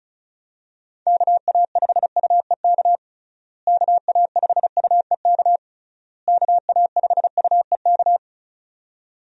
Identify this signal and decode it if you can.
Morse code, heard as a keyed tone, about 36 wpm, 700 Hz. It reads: KA5UEK KA5UEK KA5UEK
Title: USA Call Signs - 6 Characters - Repeated 3x - 35wpm